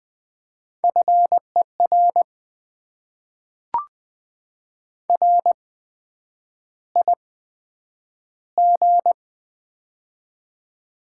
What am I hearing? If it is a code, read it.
Morse code, heard as a keyed tone, about 20 wpm, spaced as for 9 wpm, 700 Hz. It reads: FER RIG
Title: Top 20 QSO Elements - 8x Character Spacing - 20wpm